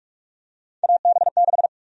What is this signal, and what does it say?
Morse code, heard as a keyed tone, about 44 wpm, 700 Hz. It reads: AB6
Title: USA Call Sign Prefixes - 45wpm